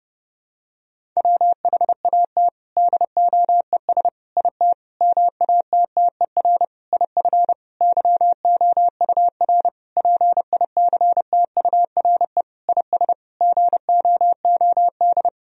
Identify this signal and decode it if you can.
Morse code, heard as a keyed tone, about 32 wpm, 700 Hz. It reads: WHAT DOES IT MATTER IF YOUR PICTURE IS GOOD
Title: Sentences from Top 1000 Words - Review Part 1 - 30wpm